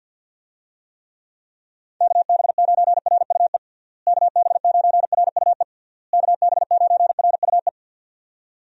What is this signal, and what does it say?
Morse code, heard as a keyed tone, about 51 wpm, 700 Hz. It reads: KB9RUE KB9RUE KB9RUE
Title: USA Call Signs - 6 Characters - Repeated 3x - 50wpm